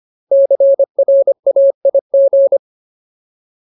Morse code, heard as a keyed tone, about 25 wpm, 550 Hz. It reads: CRAIG
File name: Ultimate Word List / 25WPM Ultimate